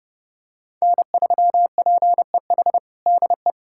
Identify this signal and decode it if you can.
Morse code, heard as a keyed tone, about 30 wpm, 700 Hz. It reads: N3PEH DE
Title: USA Callsign DE Callsign 30wpm